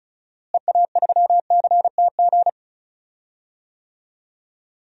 Morse code, heard as a keyed tone, about 36 wpm, 700 Hz. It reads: EA3CTG
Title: DXCC-100 Call Signs - 6 Characters - Repeated 3x - 35wpm